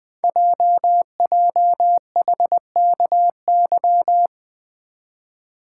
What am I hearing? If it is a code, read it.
Morse code, heard as a keyed tone, about 20 wpm, 700 Hz. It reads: JJHKY
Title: Code Groups - 5 Letters - 20wpm